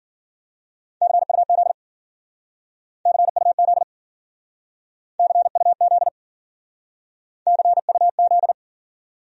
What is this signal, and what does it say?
Morse code, heard as a keyed tone, about 50 wpm, 700 Hz. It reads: CUZ CUZ CUZ CUZ
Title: Speed-Racing Abbreviations - 40wpm